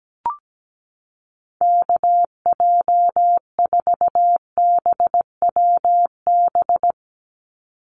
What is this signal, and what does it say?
Morse code, heard as a keyed tone, about 17 wpm, 700 Hz. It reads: KJ4BWB
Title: USA Call Signs 17wpm